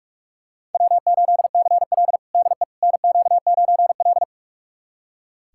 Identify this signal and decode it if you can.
Morse code, heard as a keyed tone, about 44 wpm, 690 Hz. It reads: W8CL DE NQ9L
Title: USA Callsign DE Callsign - 4 Characters - 45wpm